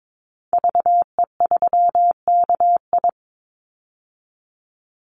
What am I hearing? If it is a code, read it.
Morse code, heard as a keyed tone, about 22 wpm, 700 Hz. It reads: VE3KI
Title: Call Signs - 22wpm